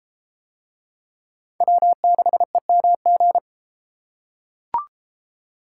Morse code, heard as a keyed tone, about 33 wpm, 700 Hz. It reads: W6EMG